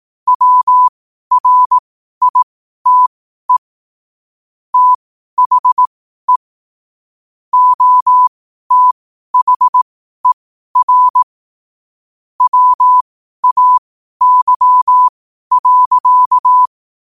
Morse code, heard as a keyed tone, about 18 wpm, spaced as for 12 wpm, 1000 Hz. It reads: WRITE THE OTHER WAY.
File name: qotdfiles / 12wpm